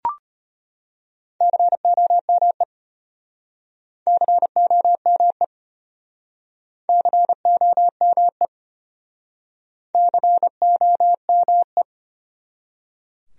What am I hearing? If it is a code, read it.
Morse code, heard as a keyed tone, about 33 wpm, 700 Hz. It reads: COME COME COME COME